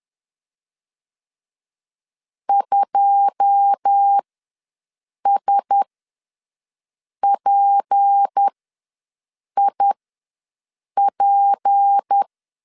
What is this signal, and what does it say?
Morse code, heard as a keyed tone, about 10 wpm, spaced as for 6 wpm, 800 Hz. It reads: 2SPIP
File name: LEZIONI / 10 SIGNS / 10A SLASH ws 10wpm 4121